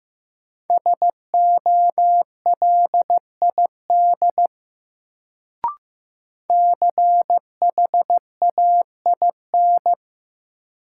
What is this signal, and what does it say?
Morse code, heard as a keyed tone, about 15 wpm, 700 Hz. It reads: SOLID CHAIN
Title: Sentences from Top 1500 Words - New Word Drill - 15wpm